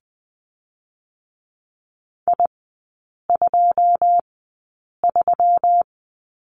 Morse code, heard as a keyed tone, about 20 wpm, spaced as for 8 wpm, 700 Hz. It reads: I23
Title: 184 Lesson 30e - Letter b - Three Characters 20wpm